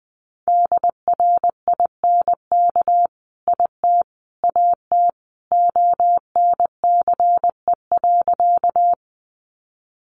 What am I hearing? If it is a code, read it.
Morse code, heard as a keyed tone, about 20 wpm, 700 Hz. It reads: DRINK IT AT ONCE.
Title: Sentences from Top 1500 Words 20wpm